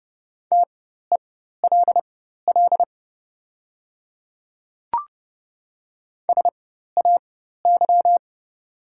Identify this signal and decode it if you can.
Morse code, heard as a keyed tone, about 30 wpm, spaced as for 14 wpm, 700 Hz. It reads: TELL SAY
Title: 4x Character Spacing - Top 200 Words - 30wpm